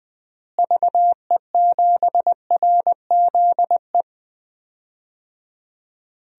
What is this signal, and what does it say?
Morse code, heard as a keyed tone, about 20 wpm, 700 Hz. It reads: VE7RZE
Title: DXCC-100 Call Signs - 6 Characters - Repeated 3x - 20wpm